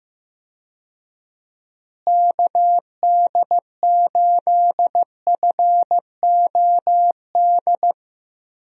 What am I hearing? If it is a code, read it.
Morse code, heard as a keyed tone, about 15 wpm, 700 Hz. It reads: KD8FOD
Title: USA Call Signs - 6 Characters - Repeated 2x - 15wpm